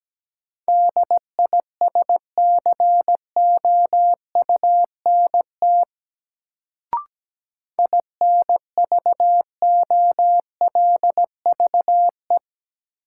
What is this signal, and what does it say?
Morse code, heard as a keyed tone, about 17 wpm, 700 Hz. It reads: DISCOUNT INVOLVE
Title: Sentences from Top 1200 Words - New Word Drill - 17wpm